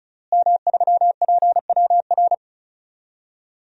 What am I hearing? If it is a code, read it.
Morse code, heard as a keyed tone, about 36 wpm, 700 Hz. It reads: M3PWR
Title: DXCC-100 Call Signs - 5 Characters - Repeated 2x - 35wpm